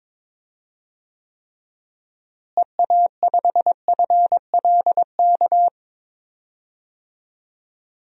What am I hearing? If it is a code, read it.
Morse code, heard as a keyed tone, about 22 wpm, 700 Hz. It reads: EA5FLK
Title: DXCC-100 Call Signs - 6 Characters - Repeated 3x - 22wpm